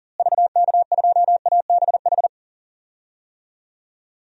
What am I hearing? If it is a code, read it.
Morse code, heard as a keyed tone, about 40 wpm, 700 Hz. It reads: VK2ABH
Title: DXCC-100 Call Signs - 6 Characters - Repeated 2x - 40wpm